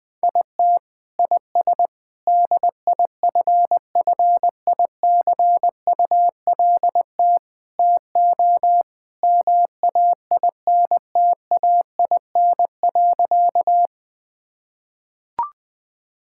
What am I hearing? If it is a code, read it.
Morse code, heard as a keyed tone, about 20 wpm, 700 Hz. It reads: IT IS DIFFICULT TO MAINTAIN.
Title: Sentences from Top 1000 Words - Review Part 5 - 20wpm